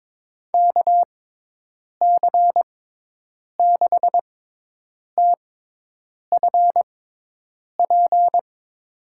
Morse code, heard as a keyed tone, about 22 wpm, spaced as for 8 wpm, 700 Hz. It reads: KC6TFP
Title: USA Call Signs - 6 Characters - 6x Character Spacing - 22wpm